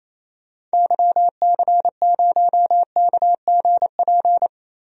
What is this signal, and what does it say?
Morse code, heard as a keyed tone, about 28 wpm, 700 Hz. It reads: YC0KGP
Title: DXCC-100 Call Signs - 6 Characters - 28wpm